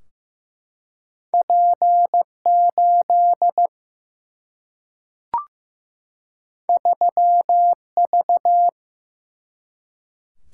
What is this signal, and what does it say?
Morse code, heard as a keyed tone, about 15 wpm, 700 Hz. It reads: P 8 3 V
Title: Single-Letter Single-Number 15wpm